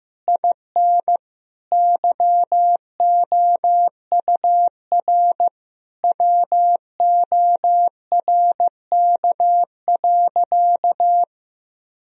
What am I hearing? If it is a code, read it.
Morse code, heard as a keyed tone, about 16 wpm, 700 Hz. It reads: IN YOUR WORK.
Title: Sentences from Top 1000 Words - Review Part 4 - 15wpm